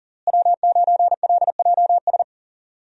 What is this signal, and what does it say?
Morse code, heard as a keyed tone, about 40 wpm, 700 Hz. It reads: W9LJS
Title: USA Call Signs - 5 Characters - Repeated 3x - 40wpm